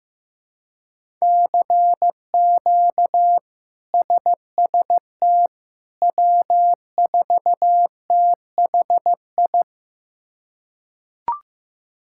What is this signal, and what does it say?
Morse code, heard as a keyed tone, about 16 wpm, 700 Hz. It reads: CQ SST W4THI
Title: K1USN SST QSO Practice - 15wpm